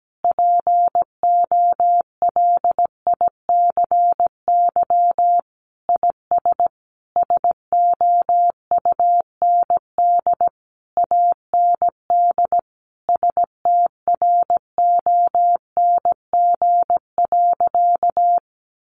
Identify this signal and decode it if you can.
Morse code, heard as a keyed tone, about 17 wpm, 700 Hz. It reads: POLICY IS SOUND AND STRONG.
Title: Sentences from Top 1400 Words 17wpm